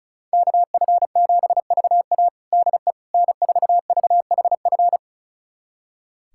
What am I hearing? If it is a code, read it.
Morse code, heard as a keyed tone, about 34 wpm, 700 Hz. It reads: KF7VA DE N4VHF
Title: USA Callsign DE Callsign - 5 Characters - 35wpm